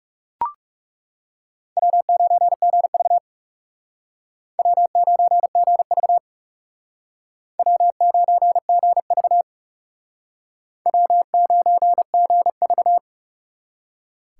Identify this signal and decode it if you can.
Morse code, heard as a keyed tone, about 40 wpm, 700 Hz. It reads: W9GV W9GV W9GV W9GV